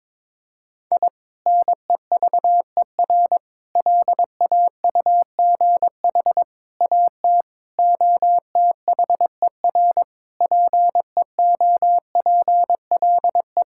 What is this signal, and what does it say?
Morse code, heard as a keyed tone, about 22 wpm, 700 Hz. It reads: I NEVER LAUGH AT OTHER PEOPLE
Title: Sentences from Top 1000 Words - Review Part 6 - 22wpm